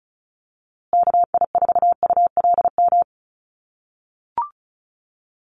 Morse code, heard as a keyed tone, about 34 wpm, 700 Hz. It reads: KI4ULM
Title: USA Call Signs Repeated 3x 35wpm